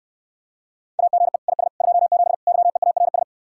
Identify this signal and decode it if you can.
Morse code, heard as a keyed tone, about 68 wpm, 700 Hz. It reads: AGE IS 27 YEARS